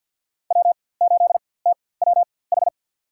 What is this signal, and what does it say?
Morse code, heard as a keyed tone, about 52 wpm, spaced as for 23 wpm, 700 Hz. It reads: W8TWH
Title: USA Call Signs - 5 Characters - 4x Character Spacing - 50wpm